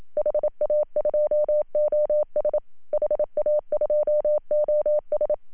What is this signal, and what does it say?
Morse code, heard as a keyed tone, about 26 wpm, 590 Hz. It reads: HA2OS HA2OS
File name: CW / HA2OS RHOB 2017 CW